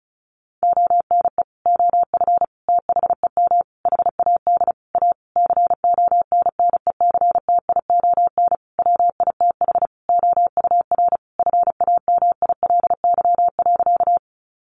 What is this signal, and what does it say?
Morse code, heard as a keyed tone, about 34 wpm, 700 Hz. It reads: ONE OF THEM HAD A CONNECTION WITH OUR FAMILY.
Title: Sentences from Top 1300 Words 35wpm